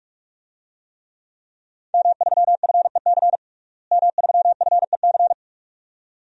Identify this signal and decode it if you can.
Morse code, heard as a keyed tone, about 44 wpm, 700 Hz. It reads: M3FEC M3FEC
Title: DXCC-100 Call Signs - 5 Characters - Repeated 2x - 45wpm